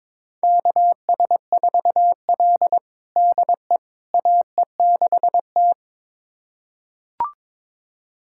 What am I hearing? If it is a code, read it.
Morse code, heard as a keyed tone, about 22 wpm, 700 Hz. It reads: KS4L DE AE6T
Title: USA Callsign DE Callsign - 4 Characters - 22wpm